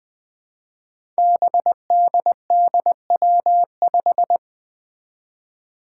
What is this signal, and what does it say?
Morse code, heard as a keyed tone, about 20 wpm, 700 Hz. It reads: BDDW5
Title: Five Challenging Letter-Number Combinations 20wpm